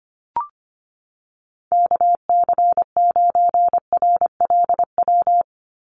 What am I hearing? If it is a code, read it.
Morse code, heard as a keyed tone, about 25 wpm, 700 Hz. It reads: KC9RLW
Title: USA Call Signs 25wpm